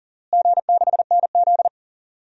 Morse code, heard as a keyed tone, about 40 wpm, 700 Hz. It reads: G6NZ
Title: DXCC-100 Call Signs - 4 Characters - Repeated 2x - 40wpm